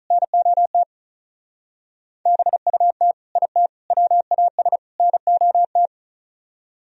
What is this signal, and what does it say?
Morse code, heard as a keyed tone, about 42 wpm, 700 Hz. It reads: NOT BUT IT WAS NOT
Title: Speed-Racing - Sets of 4 Words 35wpm